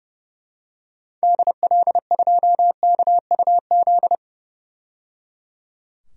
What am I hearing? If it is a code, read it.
Morse code, heard as a keyed tone, about 32 wpm, 700 Hz. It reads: DL2KUZ
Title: DXCC-100 Call Signs - 6 Characters - 30wpm